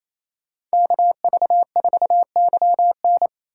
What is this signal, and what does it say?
Morse code, heard as a keyed tone, about 33 wpm, 700 Hz. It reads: KV4YN
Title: Speed Racing - USA 2x2 Call Signs 28wpm